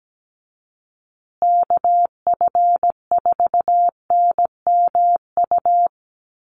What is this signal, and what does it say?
Morse code, heard as a keyed tone, about 17 wpm, 700 Hz. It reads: KF4NMU
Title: USA Call Signs Repeated 3x 17wpm